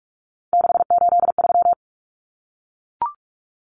Morse code, heard as a keyed tone, about 44 wpm, 700 Hz. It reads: 683
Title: Numbers - 3-Digit 45wpm